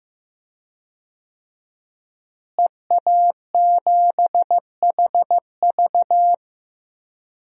Morse code, heard as a keyed tone, about 15 wpm, 700 Hz. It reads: EA7HV
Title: DXCC-100 Call Signs - 5 Characters - Repeated 2x - 15wpm